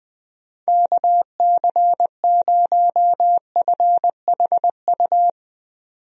Morse code, heard as a keyed tone, about 21 wpm, 700 Hz. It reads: KC0FHU
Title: USA Call Signs - 6 Characters - 20wpm